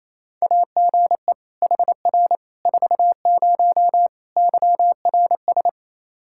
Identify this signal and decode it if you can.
Morse code, heard as a keyed tone, about 28 wpm, 700 Hz. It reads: AGE HR 40 YRS
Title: Practice QSO - Exchange 3 - More Info - 28wpm